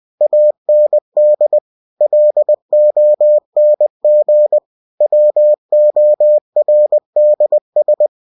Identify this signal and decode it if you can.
Morse code, heard as a keyed tone, about 20 wpm, 600 Hz. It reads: AND LONG WORDS
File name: cw-books / winnie-the-pooh / winnie-20wpm-page-0029